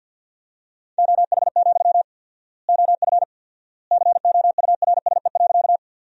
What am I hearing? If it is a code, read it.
Morse code, heard as a keyed tone, about 51 wpm, 700 Hz. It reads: OH, OF COURSE.
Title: 4x Word-Spacing - Sentences from Top 500 Words - 50wpm